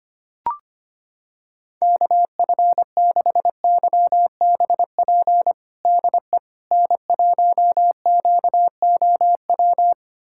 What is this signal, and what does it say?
Morse code, heard as a keyed tone, about 25 wpm, 700 Hz. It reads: KF6YBP DE N1QOW
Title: USA Callsign DE Callsign 25wpm